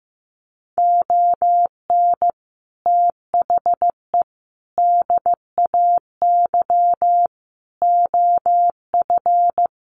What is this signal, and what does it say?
Morse code, heard as a keyed tone, about 15 wpm, 700 Hz. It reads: ON THE DAY OF
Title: Sets of 5 Words Encore 15wpm